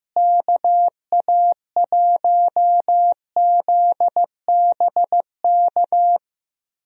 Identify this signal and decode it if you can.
Morse code, heard as a keyed tone, about 15 wpm, 700 Hz. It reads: KA1ZBK
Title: USA Call Signs - 6 Characters - Repeated 3x - 15wpm